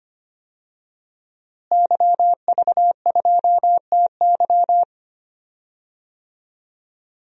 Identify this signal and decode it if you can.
Morse code, heard as a keyed tone, about 25 wpm, 700 Hz. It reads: YV2TY
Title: DXCC-100 Call Signs - 5 Characters - Repeated 3x - 25wpm